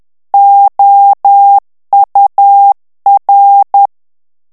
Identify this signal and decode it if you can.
Morse code, heard as a keyed tone, about 10 wpm, 800 Hz. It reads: OUR
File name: morse mp3 / ANGLAIS texte clair 2 vitesse 10 mots minute